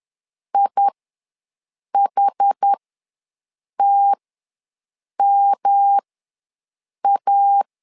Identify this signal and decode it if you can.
Morse code, heard as a keyed tone, about 10 wpm, spaced as for 6 wpm, 800 Hz. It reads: IHTMA
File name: LEZIONI / 3 MWRO / 3 M ws 10wpm 2121